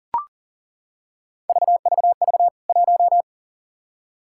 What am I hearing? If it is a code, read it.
Morse code, heard as a keyed tone, about 40 wpm, 700 Hz. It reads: VVV 1
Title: K1USN SST QSO Practice 40wpm